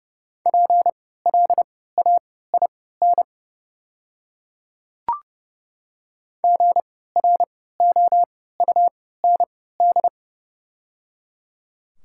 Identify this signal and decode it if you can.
Morse code, heard as a keyed tone, about 30 wpm, spaced as for 17 wpm, 690 Hz. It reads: PLAIN GROUND
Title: 3x Character Spacing - Top 500 Words - 30wpm